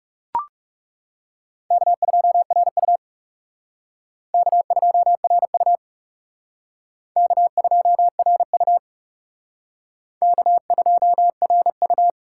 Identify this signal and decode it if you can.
Morse code, heard as a keyed tone, about 37 wpm, 700 Hz. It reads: K2RU K2RU K2RU K2RU
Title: Speed Racing - USA 1x2 Call Signs 30wpm